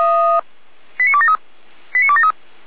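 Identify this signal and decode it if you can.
Five-tone selective call (ZVEI), heard as quick runs of five tones, tones of about 70 ms. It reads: 89273 89273
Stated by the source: ZVEI five-tone call received by SDR, archived on the Feuerwehr Orlamünde audio server (f85360).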